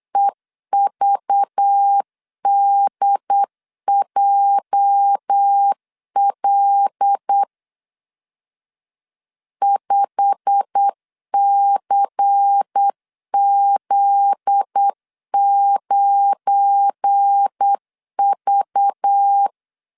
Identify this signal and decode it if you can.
Morse code, heard as a keyed tone, about 8 wpm, 800 Hz. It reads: EVDJL 5CZ9V
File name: LEZIONI / 8 YJ45 / 8 YJ45 8wpm 3214